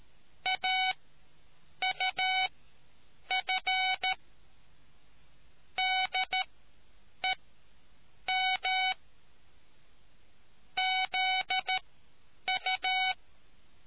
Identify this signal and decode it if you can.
Morse code, heard as a keyed tone, about 13 wpm, spaced as for 7 wpm, 750 Hz. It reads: AUF DEM ZU